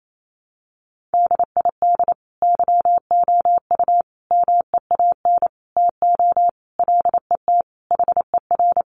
Morse code, heard as a keyed tone, about 28 wpm, 700 Hz. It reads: DID YOU MEAN TO LET HER
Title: Sentences from Top 1300 Words 28wpm